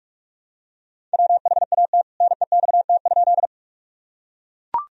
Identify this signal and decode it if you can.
Morse code, heard as a keyed tone, about 41 wpm, 700 Hz. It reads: WHAT NEXT?
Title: Sentences from Top 1000 Words - Review Part 1 - 45wpm